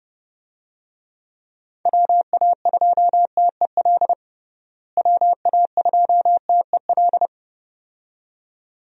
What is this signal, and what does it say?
Morse code, heard as a keyed tone, about 32 wpm, 700 Hz. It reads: WA2TEL WA2TEL
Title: USA Call Signs - 6 Characters - Repeated 2x - 30wpm